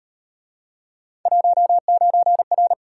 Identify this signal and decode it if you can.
Morse code, heard as a keyed tone, about 38 wpm, 700 Hz. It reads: 19R